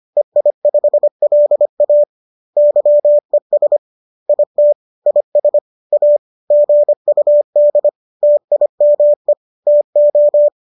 Morse code, heard as a keyed tone, about 25 wpm, 590 Hz. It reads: EI5LA YES IT IS A GUD TIME TO